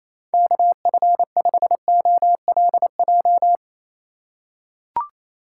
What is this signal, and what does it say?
Morse code, heard as a keyed tone, about 28 wpm, 700 Hz. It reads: KF5OLJ
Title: USA Call Signs - 6 Characters - 2x Character Spacing - 28wpm